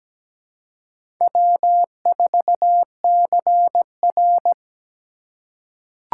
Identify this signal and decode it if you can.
Morse code, heard as a keyed tone, about 17 wpm, 700 Hz. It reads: W4CR
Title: USA Call Signs - 4 Characters - 3x Character Spacing - 17wpm